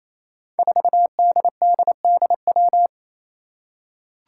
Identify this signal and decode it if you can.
Morse code, heard as a keyed tone, about 28 wpm, 700 Hz. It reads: 4DDDW
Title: Five Challenging Letter-Number Combinations 28wpm